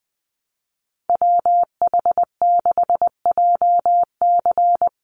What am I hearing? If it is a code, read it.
Morse code, heard as a keyed tone, about 20 wpm, 700 Hz. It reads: WH6JC
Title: USA Call Signs 20wpm